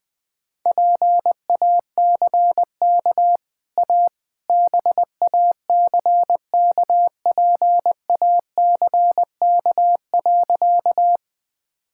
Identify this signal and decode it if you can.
Morse code, heard as a keyed tone, about 21 wpm, 700 Hz. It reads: PACK A BACKPACK.
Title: Easy Sentences with Hints - 20wpm